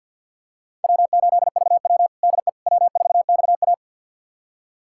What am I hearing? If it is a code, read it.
Morse code, heard as a keyed tone, about 50 wpm, 700 Hz. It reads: W8VW DE W4XA
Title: USA Callsign DE Callsign - 4 Characters - 50wpm